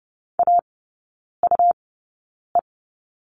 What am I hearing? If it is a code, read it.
Morse code, heard as a keyed tone, about 30 wpm, spaced as for 9 wpm, 700 Hz. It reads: AUE
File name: Lesson 18e - Letter m - Three Characters 30wpm